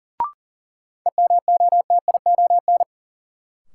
Morse code, heard as a keyed tone, about 40 wpm, 700 Hz. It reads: EMOTION
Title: Sentences from Top 1600 Words - New Word Drill - 40wpm